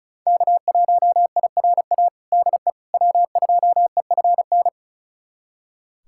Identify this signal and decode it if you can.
Morse code, heard as a keyed tone, about 34 wpm, 700 Hz. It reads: K1IRA DE W2EFN
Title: USA Callsign DE Callsign - 5 Characters - 35wpm